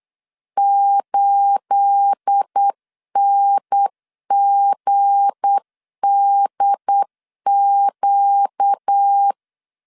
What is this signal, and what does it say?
Morse code, heard as a keyed tone, about 8 wpm, 800 Hz. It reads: 8NGDQ
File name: LEZIONI / 9 28 8wpm 3512